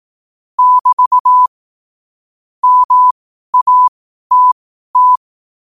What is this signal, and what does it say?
Morse code, heard as a keyed tone, about 18 wpm, spaced as for 12 wpm, 1000 Hz. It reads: <BT> MATT
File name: qotdfiles / 12wpm